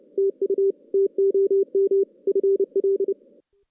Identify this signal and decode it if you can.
Morse code, heard as a keyed tone, about 25 wpm, 410 Hz. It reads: TU TOM FL
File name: 2016 JAN CW NAQP / 2016 Jan naqp k5ot - 0-6